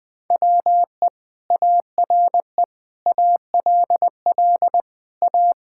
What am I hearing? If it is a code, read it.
Morse code, heard as a keyed tone, about 20 wpm, 700 Hz. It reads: WE ARE ALL A
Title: Sentences from Top 1000 Words - Review Part 6 - 20wpm